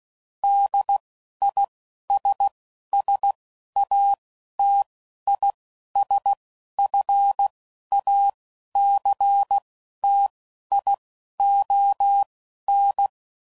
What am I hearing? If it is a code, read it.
Morse code, heard as a keyed tone, about 16 wpm, spaced as for 11 wpm, 800 Hz. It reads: DISSATISFACTION